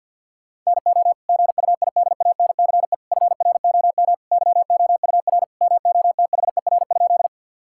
Morse code, heard as a keyed tone, about 50 wpm, 700 Hz. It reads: NO GUIDANCE FROM YOUR MOTHER?
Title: Sentences from Top 1600 Words 50wpm